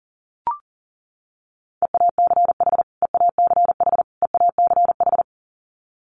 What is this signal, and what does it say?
Morse code, heard as a keyed tone, about 40 wpm, 700 Hz. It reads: EACH EACH EACH
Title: Top 500 Words Repeated 40wpm